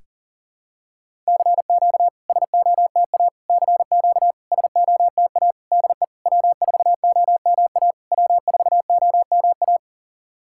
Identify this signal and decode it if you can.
Morse code, heard as a keyed tone, about 40 wpm, 690 Hz. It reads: CQ SOTA CQ SOTA DE W4OMA W4OMA
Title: SOTA QSO Practice 40wpm